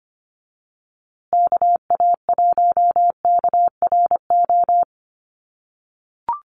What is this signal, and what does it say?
Morse code, heard as a keyed tone, about 25 wpm, 700 Hz. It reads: KA1KRO
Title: USA Call Signs Repeated 2x 25wpm